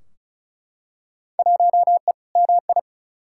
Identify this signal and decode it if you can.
Morse code, heard as a keyed tone, about 36 wpm, 700 Hz. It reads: 1E MI
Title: Field Day Exchanges - 35wpm